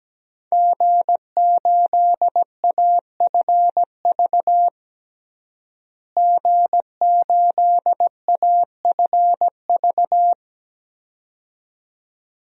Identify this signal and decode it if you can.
Morse code, heard as a keyed tone, about 17 wpm, 700 Hz. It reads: G8AFV G8AFV
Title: DXCC-100 Call Signs - 5 Characters - Repeated 2x - 17wpm